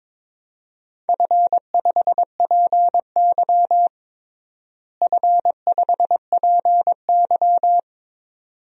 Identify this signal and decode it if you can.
Morse code, heard as a keyed tone, about 22 wpm, 700 Hz. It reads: F5PY F5PY
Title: DXCC-100 Call Signs - 4 Characters - Repeated 2x - 22wpm